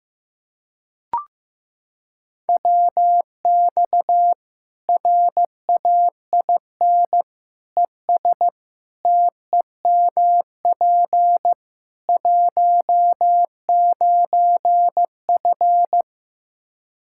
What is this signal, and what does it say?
Morse code, heard as a keyed tone, about 16 wpm, 700 Hz. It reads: WX RAIN ES TEMP 19F
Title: Practice QSO - Exchange 2 - Info - 15wpm